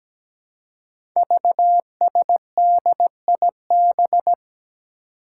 Morse code, heard as a keyed tone, about 17 wpm, 700 Hz. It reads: VSDIB
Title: Five Challenging Letter-Number Combinations 17wpm